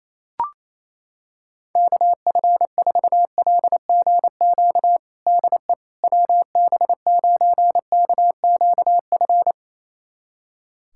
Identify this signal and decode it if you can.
Morse code, heard as a keyed tone, about 28 wpm, 700 Hz. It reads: KF4LGQ DE WB9KQF
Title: USA Callsign DE Callsign - 6 Characters - 28wpm